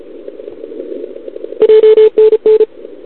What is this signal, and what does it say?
Morse code, heard as a keyed tone, about 34 wpm, 440 Hz. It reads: JNN
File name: CQWWCW2021 / OL3Z